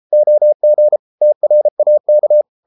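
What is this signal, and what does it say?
Morse code, heard as a keyed tone, about 33 wpm, 600 Hz. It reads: OG TRAK
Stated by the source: Morse code at 33 wpm, QRQ.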